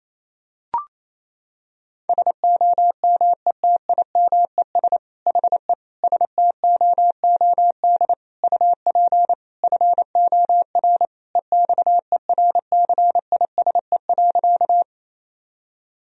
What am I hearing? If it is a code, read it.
Morse code, heard as a keyed tone, about 28 wpm, 700 Hz. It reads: SOMETIMES HE STOOD UP FOR EXERCISE.
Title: Sentences from Top 1000 Words - Review Part 4 - 28wpm